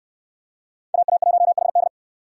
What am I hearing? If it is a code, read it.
Morse code, heard as a keyed tone, about 63 wpm, 700 Hz. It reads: AA1HD